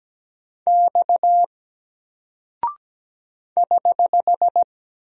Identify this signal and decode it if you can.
Morse code, heard as a keyed tone, about 17 wpm, 700 Hz. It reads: X <HH>
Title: Single Character (Letter:Number:Punctuation:Prosign) Alt 17wpm